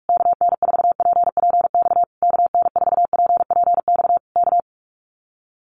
Morse code, heard as a keyed tone, about 46 wpm, 700 Hz. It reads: KN4PPX KN4PPX K
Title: Common QSO Phrases Encore 45wpm